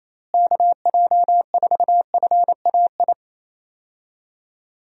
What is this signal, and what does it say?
Morse code, heard as a keyed tone, about 27 wpm, 700 Hz. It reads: KJ4FAI
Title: USA Call Signs - 6 Characters - Repeated 3x - 28wpm